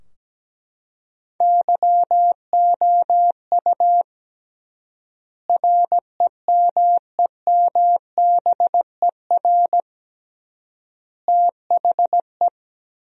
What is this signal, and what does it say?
Morse code, heard as a keyed tone, about 17 wpm, 700 Hz. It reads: YOU REMEMBER THE
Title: 3x Word-Spacing - Sentences from Top 400 Words - 17wpm